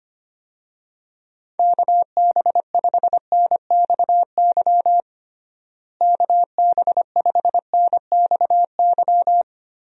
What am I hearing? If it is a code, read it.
Morse code, heard as a keyed tone, about 25 wpm, 700 Hz. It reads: KB5NXY KB5NXY
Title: USA Call Signs - 6 Characters - Repeated 3x - 25wpm